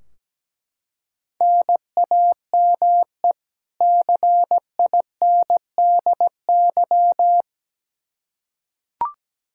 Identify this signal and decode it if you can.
Morse code, heard as a keyed tone, about 17 wpm, 700 Hz. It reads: NAME CINDY
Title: K1USN SST QSO Practice 17wpm